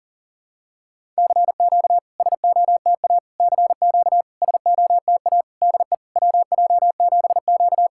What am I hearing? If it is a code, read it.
Morse code, heard as a keyed tone, about 41 wpm, 700 Hz. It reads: CQ SOTA CQ SOTA DE WJ7Q
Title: SOTA QSO Practice 40wpm